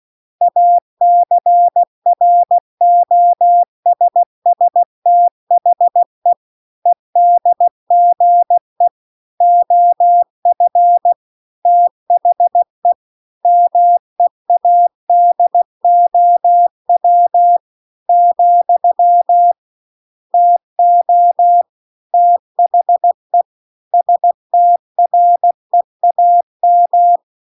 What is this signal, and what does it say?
Morse code, heard as a keyed tone, about 16 wpm, 700 Hz. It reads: ACROSS THE EDGE OF THE MEADOW , TO THE STREAM